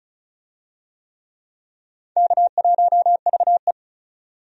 Morse code, heard as a keyed tone, about 36 wpm, 700 Hz. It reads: K1VE
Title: USA Call Signs - 4 Characters - 35wpm